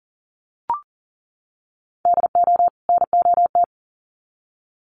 Morse code, heard as a keyed tone, about 40 wpm, 700 Hz. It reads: DO NOT
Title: Sets of 2 Words 40wpm